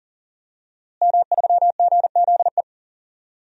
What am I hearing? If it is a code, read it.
Morse code, heard as a keyed tone, about 40 wpm, 700 Hz. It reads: M3GZE
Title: DXCC-100 Call Signs - 5 Characters - Repeated 2x - 40wpm